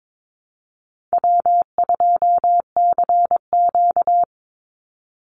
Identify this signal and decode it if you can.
Morse code, heard as a keyed tone, about 22 wpm, 700 Hz. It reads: W2CQ
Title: Easy Call Signs Repeated 2x - 22wpm